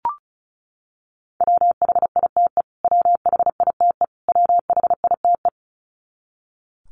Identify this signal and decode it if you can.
Morse code, heard as a keyed tone, about 34 wpm, 700 Hz. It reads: WHITE WHITE WHITE
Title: Top 300 Words Repeated 35wpm